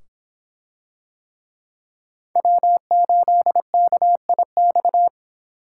Morse code, heard as a keyed tone, about 26 wpm, 700 Hz. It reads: W8KIX